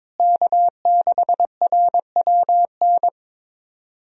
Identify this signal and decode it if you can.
Morse code, heard as a keyed tone, about 22 wpm, 700 Hz. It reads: K6RWN
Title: USA Call Signs - 5 Characters - Repeated 2x - 22wpm